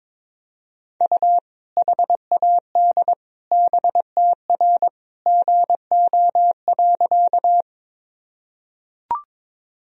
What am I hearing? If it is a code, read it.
Morse code, heard as a keyed tone, about 22 wpm, 700 Hz. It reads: U HAD BTR GO.
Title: Sentences from Top 400 Words and Abbreviations 22wpm